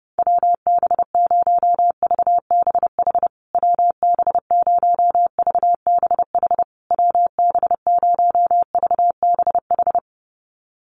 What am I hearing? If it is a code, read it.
Morse code, heard as a keyed tone, about 30 wpm, 700 Hz. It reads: WB0VBH WB0VBH WB0VBH
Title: USA Call Signs Repeated 3x 30wpm